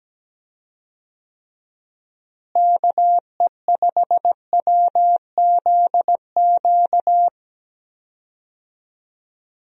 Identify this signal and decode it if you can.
Morse code, heard as a keyed tone, about 17 wpm, 700 Hz. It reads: KE5WZQ
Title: USA Call Signs - 6 Characters - Repeated 2x - 17wpm